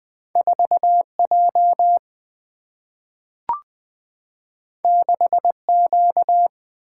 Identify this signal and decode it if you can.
Morse code, heard as a keyed tone, about 20 wpm, 700 Hz. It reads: 4J 6Q
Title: Single-Letter Single-Number 20wpm